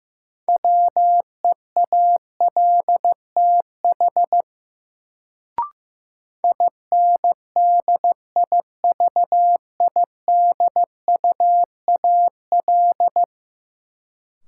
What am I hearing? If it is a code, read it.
Morse code, heard as a keyed tone, about 15 wpm, 700 Hz. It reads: WEALTH INDIVIDUAL
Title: Sentences from Top 1000 Words - Review Part 4 - New Word Drill - 15wpm